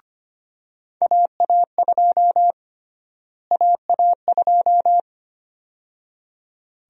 Morse code, heard as a keyed tone, about 25 wpm, 700 Hz. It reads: AA2 AA2
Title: USA Call Sign Prefixes - Repeated 2x - 25wpm